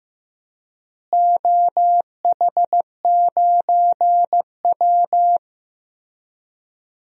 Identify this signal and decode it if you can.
Morse code, heard as a keyed tone, about 16 wpm, 700 Hz. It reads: OH9W
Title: DXCC-100 Call Signs - 4 Characters - Repeated 3x - 15wpm